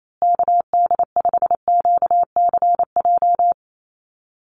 Morse code, heard as a keyed tone, about 28 wpm, 700 Hz. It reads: KD5QCJ
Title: USA Call Signs Repeated 3x 28wpm